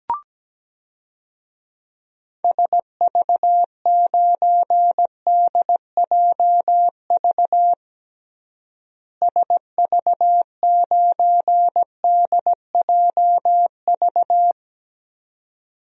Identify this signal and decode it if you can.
Morse code, heard as a keyed tone, about 17 wpm, 700 Hz. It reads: SV9DJV SV9DJV
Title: DXCC-100 Call Signs - 6 Characters - Repeated 2x - 17wpm